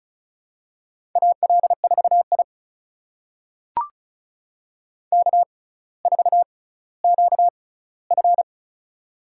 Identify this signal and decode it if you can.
Morse code, heard as a keyed tone, about 34 wpm, spaced as for 18 wpm, 700 Hz. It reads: AL4I K4QF
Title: USA Call Signs - 4 Characters - 6x Character Spacing - 35wpm